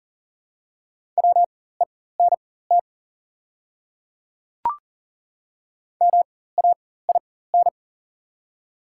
Morse code, heard as a keyed tone, about 37 wpm, spaced as for 18 wpm, 700 Hz. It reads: WENT MAIN
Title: 4x Character Spacing - Top 500 Words - 40wpm